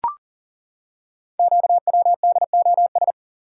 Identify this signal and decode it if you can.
Morse code, heard as a keyed tone, about 42 wpm, 690 Hz. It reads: QWDOS
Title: Code Groups - 5 Letters - 40wpm